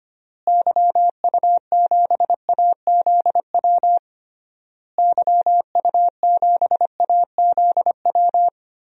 Morse code, heard as a keyed tone, about 25 wpm, 700 Hz. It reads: YU7AZW YU7AZW
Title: DXCC-100 Call Signs - 6 Characters - Repeated 3x - 25wpm